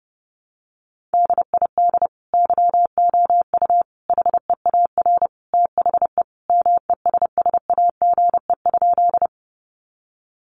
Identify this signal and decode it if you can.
Morse code, heard as a keyed tone, about 30 wpm, 700 Hz. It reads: DID YOU HEAR THE MESSAGE?
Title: Sentences from Top 1100 Words 30wpm